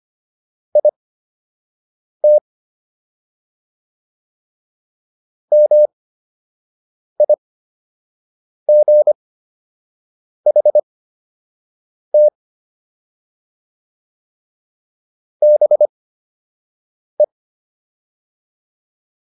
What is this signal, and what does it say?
Morse code, heard as a keyed tone, about 25 wpm, spaced as for 6 wpm, 600 Hz. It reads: IT MIGHT BE